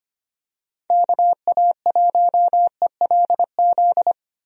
Morse code, heard as a keyed tone, about 25 wpm, 700 Hz. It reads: KA1ELZ
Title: USA Call Signs - 6 Characters - 25wpm